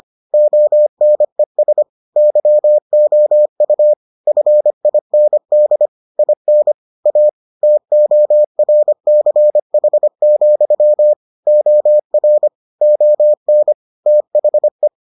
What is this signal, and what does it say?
Morse code, heard as a keyed tone, about 25 wpm, 600 Hz. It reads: ONES YOU FIND IN A TORCH, OR ON THE